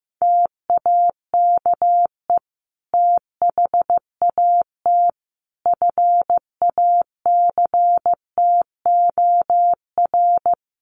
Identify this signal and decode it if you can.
Morse code, heard as a keyed tone, about 15 wpm, 700 Hz. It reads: TAKE THAT FACTOR
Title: Sentences from Top 1300 Words 15wpm